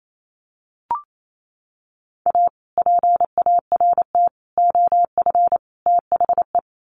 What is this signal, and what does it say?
Morse code, heard as a keyed tone, about 28 wpm, 700 Hz. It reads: A PART OF THE
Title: Sets of 4 Words 28wpm